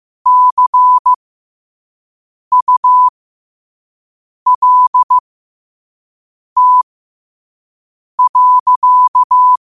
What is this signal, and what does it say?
Morse code, heard as a keyed tone, about 15 wpm, spaced as for 5 wpm, 1000 Hz. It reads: CULT.
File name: qotdfiles / 5wpm